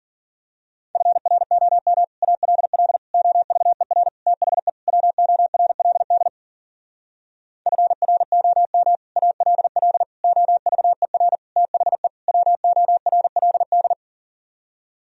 Morse code, heard as a keyed tone, about 41 wpm, 700 Hz. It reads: FROM ALL OVER THE WORLD FROM ALL OVER THE WORLD
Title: Speed-Racing - Sets of 5 Words 40wpm